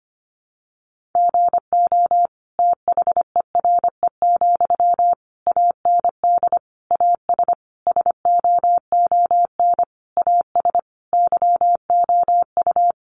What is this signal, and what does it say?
Morse code, heard as a keyed tone, about 25 wpm, 700 Hz. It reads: GO THERE, AND AS SOON AS YOU
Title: Sentences from Top 1300 Words 25wpm